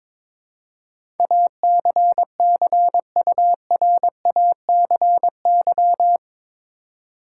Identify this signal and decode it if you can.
Morse code, heard as a keyed tone, about 22 wpm, 700 Hz. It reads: ACCURACY
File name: practice / 22wpm / Eight Letter Words - 22wpm